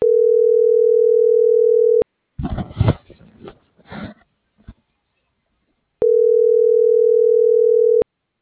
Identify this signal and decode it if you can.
Ringback tone (North America), 440 + 480 Hz.